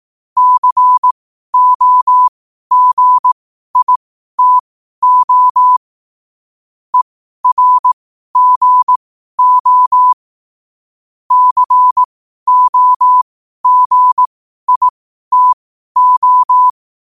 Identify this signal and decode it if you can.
Morse code, heard as a keyed tone, about 18 wpm, spaced as for 12 wpm, 1000 Hz. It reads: COGITO ERGO COGITO
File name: qotdfiles / 12wpm